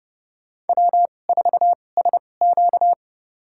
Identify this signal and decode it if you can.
Morse code, heard as a keyed tone, about 30 wpm, spaced as for 22 wpm, 700 Hz. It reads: W4SQ
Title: USA Call Signs - 4 Characters - 2x Character Spacing - 30wpm